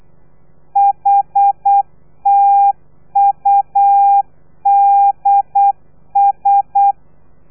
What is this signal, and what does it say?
Morse code, heard as a keyed tone, about 8 wpm, 800 Hz. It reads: HTUDS
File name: telegrafia / i2tfj / Lez3 35